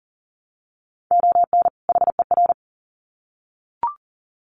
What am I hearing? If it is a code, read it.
Morse code, heard as a keyed tone, about 40 wpm, 700 Hz. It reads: ON HER
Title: Sets of 2 Words 40wpm